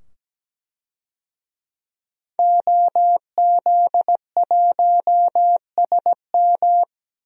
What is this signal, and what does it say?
Morse code, heard as a keyed tone, about 17 wpm, 700 Hz. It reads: OZ1SM
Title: DXCC-100 Call Signs - 5 Characters - 17wpm